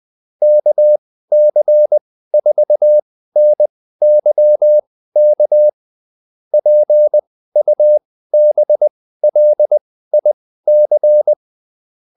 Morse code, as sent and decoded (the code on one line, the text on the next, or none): -.- -.-. ....- -. -.-- -.- / .--. ..- -... .-.. .. -.-.
KC4NYK PUBLIC